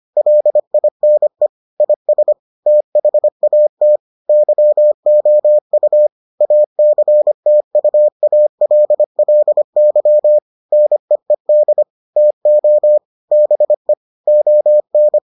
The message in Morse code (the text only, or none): LINE IS THAT YOU ACTUALLY NEED TO BE ON